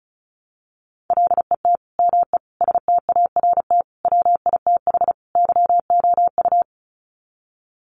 Morse code, heard as a keyed tone, about 34 wpm, 700 Hz. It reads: LET ME START WITH YOU
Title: Sets of 5 Words Encore 35wpm